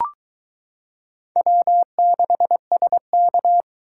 Morse code, heard as a keyed tone, about 22 wpm, 700 Hz. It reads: W6SK